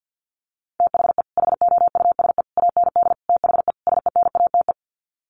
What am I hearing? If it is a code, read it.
Morse code, heard as a keyed tone, about 50 wpm, 700 Hz. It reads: THE HOUSE AND THE SENATE
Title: Sets of 5 Words Encore2 50wpm